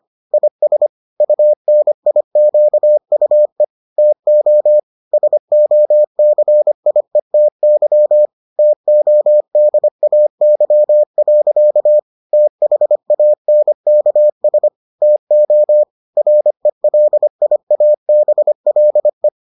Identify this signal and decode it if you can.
Morse code, heard as a keyed tone, about 25 wpm, 600 Hz. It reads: IS UNIQUE TO SOCIETY TODAY. THANKS TO RELIABLE